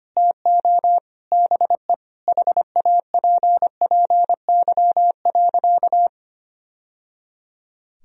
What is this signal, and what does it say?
Morse code, heard as a keyed tone, about 25 wpm, 700 Hz. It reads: TO BE HAPPY.
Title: Sentences from Top 1000 Words - Review Part 1 - 25wpm